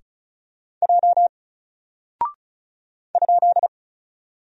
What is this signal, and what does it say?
Morse code, heard as a keyed tone, about 34 wpm, 700 Hz. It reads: J ?
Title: Single Character (Letter:Number:Punctuation:Prosign) 35wpm